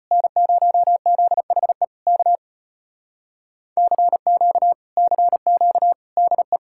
Morse code, heard as a keyed tone, about 33 wpm, 700 Hz. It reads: N0ZHE K CQ CQ DE